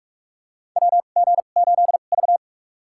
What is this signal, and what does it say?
Morse code, heard as a keyed tone, about 43 wpm, spaced as for 32 wpm, 700 Hz. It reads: WG8V
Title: USA Call Signs - 2x Character Spacing - 45wpm